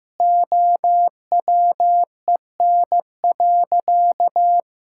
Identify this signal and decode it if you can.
Morse code, heard as a keyed tone, about 16 wpm, 700 Hz. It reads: OWEN.
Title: Pangram Challenge v2 - 15wpm